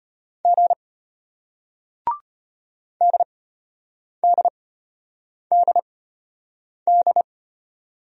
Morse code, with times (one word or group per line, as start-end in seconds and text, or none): 0.5-0.7 G
3.0-7.2 DDDD